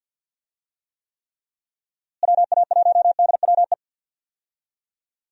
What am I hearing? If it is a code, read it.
Morse code, heard as a keyed tone, about 46 wpm, 700 Hz. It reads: WA1DWE
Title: USA Call Signs - 6 Characters - Repeated 3x - 50wpm